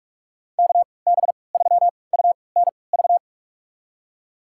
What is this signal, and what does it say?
Morse code, heard as a keyed tone, about 44 wpm, spaced as for 25 wpm, 700 Hz. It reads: KB3UNV
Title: USA Call Signs - 6 Characters - 3x Character Spacing - 45wpm